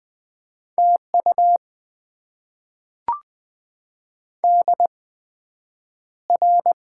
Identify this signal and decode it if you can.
Morse code, heard as a keyed tone, about 20 wpm, spaced as for 9 wpm, 700 Hz. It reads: TU DR
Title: Top 90 QSO Elements - 8x Character Spacing - 20wpm